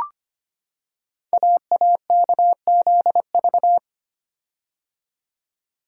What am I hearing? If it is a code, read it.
Morse code, heard as a keyed tone, about 25 wpm, 700 Hz. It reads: AAKZV
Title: Code Groups - 5 Letter:Numbers - 25wpm